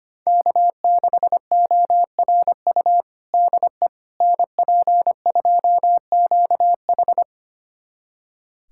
Morse code, heard as a keyed tone, about 25 wpm, 700 Hz. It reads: K6ORU DE NP2QH
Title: USA Callsign DE Callsign - 5 Characters - 25wpm